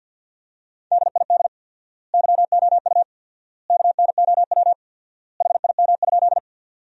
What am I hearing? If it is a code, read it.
Morse code, heard as a keyed tone, about 50 wpm, 700 Hz. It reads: DID YOU KNOW HIM?
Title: 4x Word-Spacing - Sentences from Top 100 Words - 50wpm